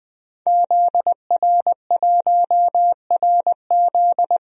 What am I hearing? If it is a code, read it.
Morse code, heard as a keyed tone, about 20 wpm, 700 Hz. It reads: ZR1RZ
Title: DXCC-100 Call Signs - 5 Characters - Repeated 3x - 20wpm